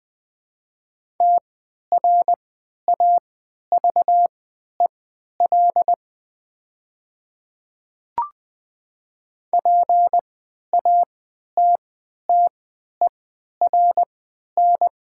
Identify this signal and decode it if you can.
Morse code, heard as a keyed tone, about 20 wpm, spaced as for 11 wpm, 700 Hz. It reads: TRAVEL PATTERN
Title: 3x Character Spacing - Top 500 Words - 20wpm